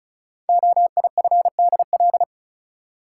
Morse code, heard as a keyed tone, about 36 wpm, 700 Hz. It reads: OIFDL
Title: Code Groups - 5 Letters - 35wpm